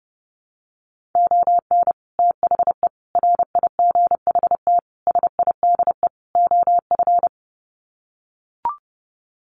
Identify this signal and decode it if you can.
Morse code, heard as a keyed tone, about 30 wpm, 700 Hz. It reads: ON THE RIGHT SIDE OF
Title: Sets of 5 Words Encore 30wpm